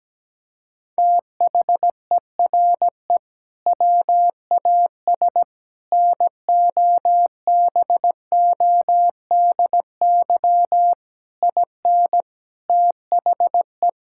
Morse code, heard as a keyed tone, about 17 wpm, 700 Hz. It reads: THERE WAS NOBODY IN THE
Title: Sentences from Top 1600 Words 17wpm